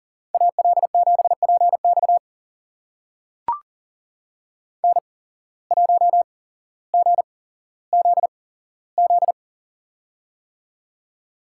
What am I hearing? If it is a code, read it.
Morse code, heard as a keyed tone, about 40 wpm, spaced as for 17 wpm, 700 Hz. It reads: AL7PX N1GZZ
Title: USA Call Signs - 5 Characters - 8x Character Spacing - 40wpm